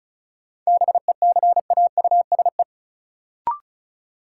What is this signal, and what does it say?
Morse code, heard as a keyed tone, about 36 wpm, 700 Hz. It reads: BECAUSE
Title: Sets of 2 Words - New Word Drill - 35wpm